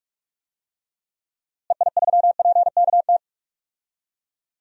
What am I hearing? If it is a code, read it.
Morse code, heard as a keyed tone, about 46 wpm, 700 Hz. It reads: EI3PKT
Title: DXCC-100 Call Signs - 6 Characters - Repeated 2x - 45wpm